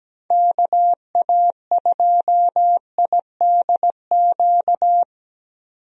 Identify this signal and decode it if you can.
Morse code, heard as a keyed tone, about 17 wpm, 700 Hz. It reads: KA2IDQ
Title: USA Call Signs - 6 Characters - 17wpm